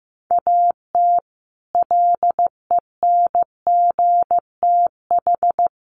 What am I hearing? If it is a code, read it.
Morse code, heard as a keyed tone, about 15 wpm, 700 Hz. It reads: AT LENGTH
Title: Sentences from Top 1000 Words 15wpm